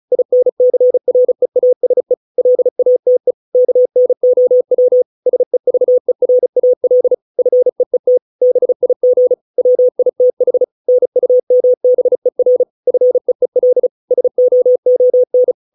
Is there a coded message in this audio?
Yes: Morse code from 0.1 to 15.5 s, heard as a keyed tone, about 34 wpm, 500 Hz. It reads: INCREASE LATE KNOW SEVERAL FEET BIG WITH NUMBER FEEL SOON